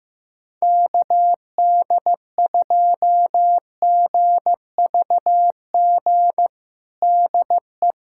Morse code, heard as a keyed tone, about 15 wpm, 700 Hz. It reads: KD2GVG DE
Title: USA Callsign DE Callsign - 6 Characters - 15wpm